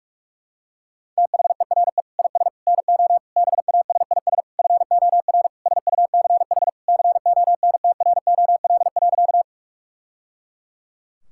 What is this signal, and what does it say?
Morse code, heard as a keyed tone, about 46 wpm, 700 Hz. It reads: THERE IS NO BASIS FOR SUCH CONTROL.
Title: Sentences from Top 1000 Words - Review Part 5 - 45wpm